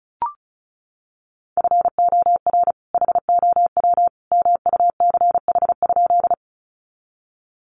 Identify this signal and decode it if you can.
Morse code, heard as a keyed tone, about 34 wpm, 700 Hz. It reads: FOR HOW MUCH?
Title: Sentences from Top 200 Words 35wpm